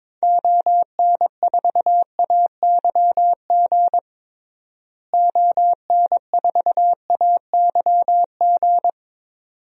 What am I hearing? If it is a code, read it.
Morse code, heard as a keyed tone, about 22 wpm, 700 Hz. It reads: ON4AYG ON4AYG
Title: DXCC-100 Call Signs - 6 Characters - Repeated 2x - 22wpm